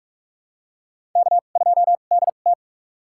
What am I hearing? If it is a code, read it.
Morse code, heard as a keyed tone, about 46 wpm, spaced as for 32 wpm, 690 Hz. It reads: K2DT
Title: USA Call Signs - 4 Characters - 2x Character Spacing - 45wpm